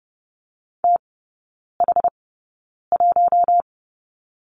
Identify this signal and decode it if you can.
Morse code, heard as a keyed tone, about 30 wpm, spaced as for 9 wpm, 700 Hz. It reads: TH1
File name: 081 Lesson 14e - Number 2 - Three Characters 30wpm